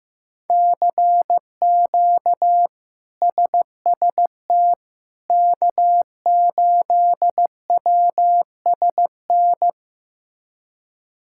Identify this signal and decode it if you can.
Morse code, heard as a keyed tone, about 16 wpm, 700 Hz. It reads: CQ SST K8WSN
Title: K1USN SST QSO Practice - 15wpm